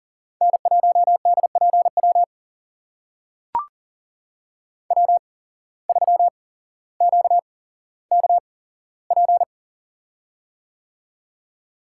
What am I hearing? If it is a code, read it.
Morse code, heard as a keyed tone, about 40 wpm, spaced as for 17 wpm, 700 Hz. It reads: N1DPW W3QKP
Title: USA Call Signs - 5 Characters - 8x Character Spacing - 40wpm